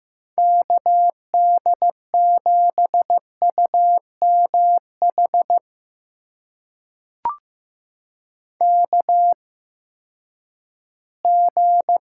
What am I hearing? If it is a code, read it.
Morse code, heard as a keyed tone, about 15 wpm, spaced as for 10 wpm, 700 Hz. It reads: KD7UMH KG
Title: USA Call Signs - 6 Characters - 8x Character Spacing - 15wpm